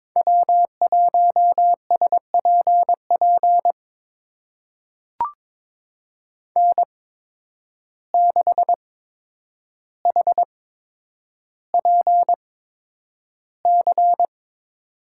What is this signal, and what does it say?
Morse code, heard as a keyed tone, about 22 wpm, spaced as for 9 wpm, 700 Hz. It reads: W1SPP N6HPC
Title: USA Call Signs - 5 Characters - 8x Character Spacing - 22wpm